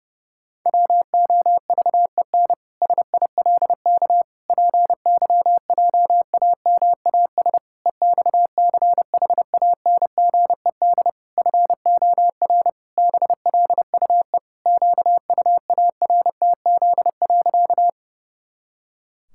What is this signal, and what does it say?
Morse code, heard as a keyed tone, about 30 wpm, 700 Hz. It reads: WOVEN SILK PYJAMAS EXCHANGED FOR BLUE QUARTZ.
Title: Pangram Challenge v2 - 30wpm